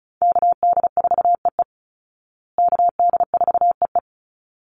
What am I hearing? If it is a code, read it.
Morse code, heard as a keyed tone, about 34 wpm, 700 Hz. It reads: KD4EE KD4EE
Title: Call Signs Repeated 3x - 35wpm